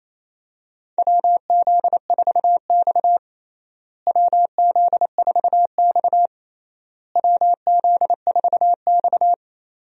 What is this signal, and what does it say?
Morse code, heard as a keyed tone, about 28 wpm, 700 Hz. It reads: WZ4X WZ4X WZ4X
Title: USA Call Signs - 4 Characters - Repeated 3x - 28wpm